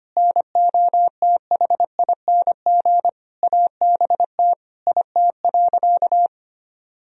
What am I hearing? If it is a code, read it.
Morse code, heard as a keyed tone, about 25 wpm, 700 Hz. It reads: NOTHING ABT IT.
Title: Sentences from Top 500 Words and Abbreviations 25wpm